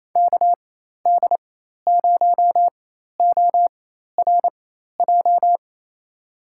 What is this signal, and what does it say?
Morse code, heard as a keyed tone, about 28 wpm, spaced as for 13 wpm, 700 Hz. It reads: KD0ORJ
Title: USA Call Signs - 6 Characters - 4x Character Spacing - 28wpm